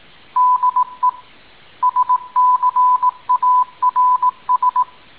Morse code, heard as a keyed tone, about 18 wpm, 1000 Hz. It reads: DE SCARS